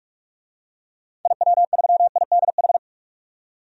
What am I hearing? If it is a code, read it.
Morse code, heard as a keyed tone, about 46 wpm, 700 Hz. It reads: IW3IDH
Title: DXCC-100 Call Signs - 6 Characters - 45wpm